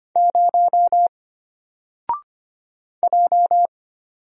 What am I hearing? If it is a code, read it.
Morse code, heard as a keyed tone, about 25 wpm, 700 Hz. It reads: 0 J
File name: Single-Letter-Number-25wpm